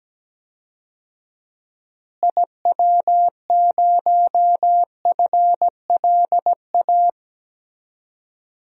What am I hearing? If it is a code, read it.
Morse code, heard as a keyed tone, about 17 wpm, 700 Hz. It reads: IW0FLA
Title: DXCC-100 Call Signs - 6 Characters - Repeated 3x - 17wpm